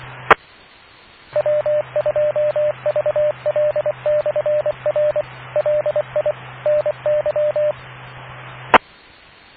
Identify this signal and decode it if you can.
Morse code, heard as a keyed tone, about 24 wpm, 600 Hz. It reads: E W2VL/R LI NY E